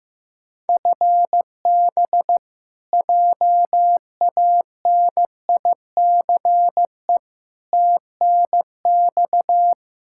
Morse code, heard as a keyed tone, about 16 wpm, 700 Hz. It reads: FB JANICE TNX
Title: K1USN SST QSO Practice 15wpm